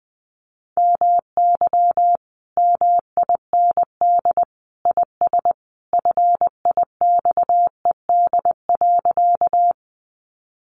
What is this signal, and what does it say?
Morse code, heard as a keyed tone, about 20 wpm, 700 Hz. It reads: MY MIND IS FIXED.
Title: Sentences from Top 1100 Words 20wpm